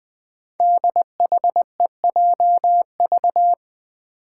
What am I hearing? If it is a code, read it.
Morse code, heard as a keyed tone, about 21 wpm, 700 Hz. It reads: DHEJV
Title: Five Challenging Letter-Number Combinations 20wpm